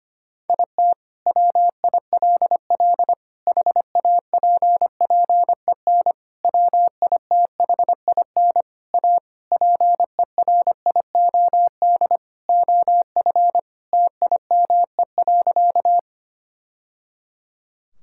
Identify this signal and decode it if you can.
Morse code, heard as a keyed tone, about 25 wpm, 700 Hz. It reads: IT WILL HAPPEN WITHIN A PERIOD OF TIME.
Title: Sentences from Top 1000 Words - Review Part 5 - 25wpm